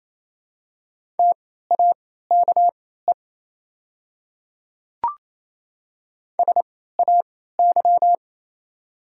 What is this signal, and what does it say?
Morse code, heard as a keyed tone, about 28 wpm, spaced as for 16 wpm, 700 Hz. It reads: TAKE SAY
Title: 3x Character Spacing - Top 500 Words - 28wpm